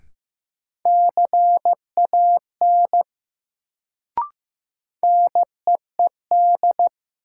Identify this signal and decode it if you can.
Morse code, heard as a keyed tone, about 15 wpm, 700 Hz. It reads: CAN NEED
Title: Sets of 2 Words - New Word Drill - 15wpm